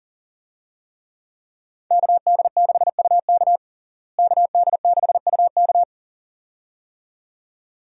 Morse code, heard as a keyed tone, about 40 wpm, 700 Hz. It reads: KD6UK KD6UK
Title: USA Call Signs - 5 Characters - Repeated 2x - 40wpm